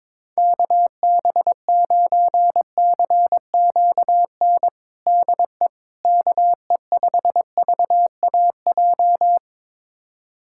Morse code, heard as a keyed tone, about 22 wpm, 700 Hz. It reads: KB9CQN DE KE5VAJ
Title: USA Callsign DE Callsign - 6 Characters - 22wpm